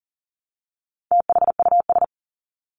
Morse code, heard as a keyed tone, about 40 wpm, 700 Hz. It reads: THUS
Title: Top 751-1000 Words 40wpm